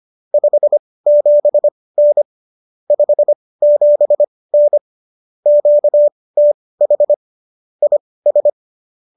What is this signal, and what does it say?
Morse code, heard as a keyed tone, about 25 wpm, spaced as for 18 wpm, 590 Hz. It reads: 57N 57N QTH IS